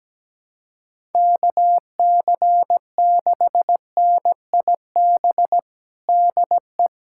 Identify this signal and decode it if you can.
Morse code, heard as a keyed tone, about 17 wpm, 700 Hz. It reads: KC6NIB DE
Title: USA Callsign DE Callsign - 6 Characters - 17wpm